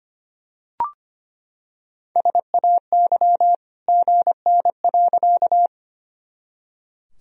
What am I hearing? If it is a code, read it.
Morse code, heard as a keyed tone, about 25 wpm, 700 Hz. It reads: SAY GN.
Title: Sentences from Top 300 Words and Abbreviations 25wpm